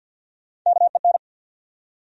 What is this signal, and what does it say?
Morse code, heard as a keyed tone, about 47 wpm, 700 Hz. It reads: KEN
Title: USA Common Names - Max 3 Letters - 50wpm